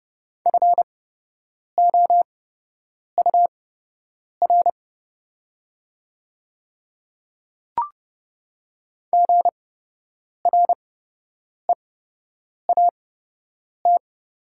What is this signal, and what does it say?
Morse code, heard as a keyed tone, about 30 wpm, spaced as for 8 wpm, 700 Hz. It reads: FOUR GREAT
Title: 8x Character Spacing - Top 300 Words - 30wpm